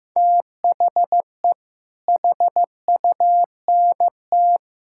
Morse code, heard as a keyed tone, about 15 wpm, 700 Hz. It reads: THE HUNT
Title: Sentences from Top 1600 Words 15wpm